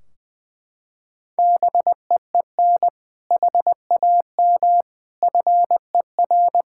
Morse code, heard as a keyed tone, about 20 wpm, 700 Hz. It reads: BEEN HAM FER
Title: Practice QSO - Exchange 3 - More Info - 20wpm